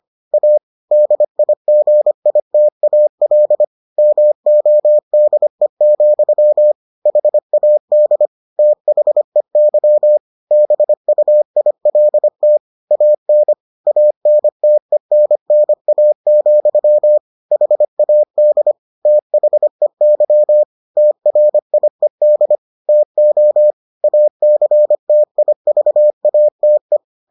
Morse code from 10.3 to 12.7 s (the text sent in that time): BUILT